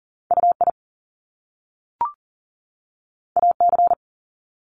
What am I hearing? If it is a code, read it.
Morse code, heard as a keyed tone, about 39 wpm, 700 Hz. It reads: UI AC
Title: Top Two Letter Combinations in Words 40wpm